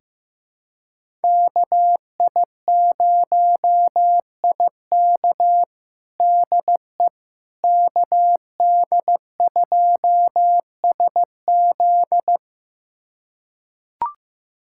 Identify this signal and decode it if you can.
Morse code, heard as a keyed tone, about 15 wpm, 700 Hz. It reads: KI0IK DE KD2SZ
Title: USA Callsign DE Callsign - 5 Characters - 15wpm